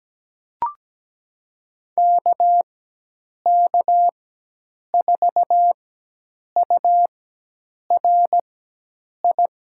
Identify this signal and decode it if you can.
Morse code, heard as a keyed tone, about 17 wpm, spaced as for 8 wpm, 700 Hz. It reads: KK4URI
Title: USA Call Signs - 4x Character Spacing - 17wpm